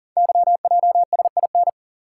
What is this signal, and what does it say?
Morse code, heard as a keyed tone, about 40 wpm, 700 Hz. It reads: YJSIN
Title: Code Groups - 5 Letters - 40wpm